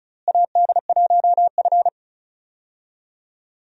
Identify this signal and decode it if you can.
Morse code, heard as a keyed tone, about 36 wpm, 700 Hz. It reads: AD1F
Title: USA Call Signs - 4 Characters - Repeated 2x - 35wpm